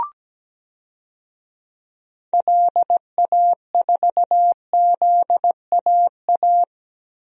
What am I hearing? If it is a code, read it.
Morse code, heard as a keyed tone, about 17 wpm, 700 Hz. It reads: LA4ZAA
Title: DXCC-100 Call Signs - 6 Characters - Repeated 2x - 17wpm